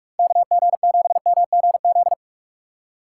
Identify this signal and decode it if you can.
Morse code, heard as a keyed tone, about 44 wpm, 700 Hz. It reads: KG7MGZ
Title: USA Call Signs - 6 Characters - 3x Character Spacing - 45wpm